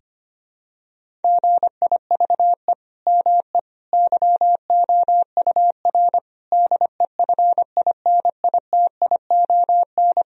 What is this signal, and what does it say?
Morse code, heard as a keyed tone, about 25 wpm, 700 Hz. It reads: GIVE ME YOUR DEFINITION
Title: Sentences from Top 1000 Words - Review Part 4 - 25wpm